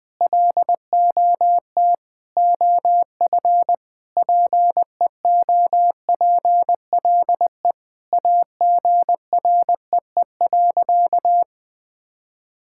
Morse code, heard as a keyed tone, about 20 wpm, 700 Hz. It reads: LOT OF PEOPLE AGREE.
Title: Sentences with 4-Word Phrases - 20wpm